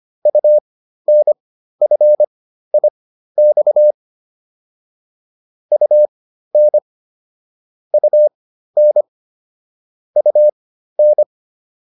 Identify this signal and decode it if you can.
Morse code, heard as a keyed tone, about 25 wpm, spaced as for 13 wpm, 600 Hz. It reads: UNFIX UN UN UN
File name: UN5 13